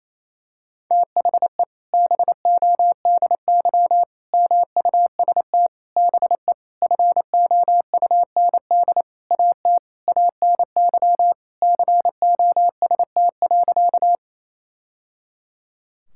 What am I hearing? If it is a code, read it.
Morse code, heard as a keyed tone, about 28 wpm, 700 Hz. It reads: THE BODY MUST BE FOUND AT ANY COST.
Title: Sentences from Top 1000 Words - Review Part 5 - 28wpm